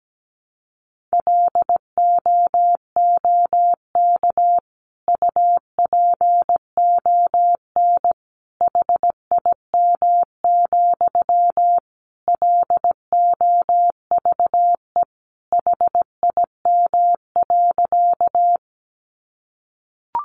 Morse code, heard as a keyed tone, about 17 wpm, 700 Hz. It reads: LOOK UPON HIM, LOVE HIM.
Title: Sentences from Top 1100 Words 17wpm